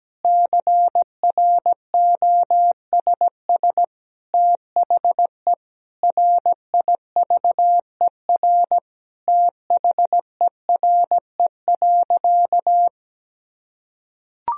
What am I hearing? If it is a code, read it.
Morse code, heard as a keyed tone, about 17 wpm, 700 Hz. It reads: CROSS THE RIVER THERE.
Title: Sentences from Top 1000 Words - Review Part 3 - 17wpm